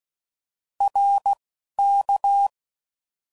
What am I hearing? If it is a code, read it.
Morse code, heard as a keyed tone, about 16 wpm, 800 Hz. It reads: R K
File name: Mixed / Short Runs / ShortContrasts